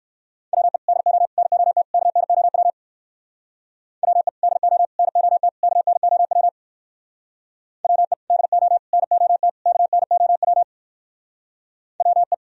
Morse code, heard as a keyed tone, about 59 wpm, 700 Hz. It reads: WE DO NOT KNOW WE DO NOT KNOW WE DO NOT KNOW WE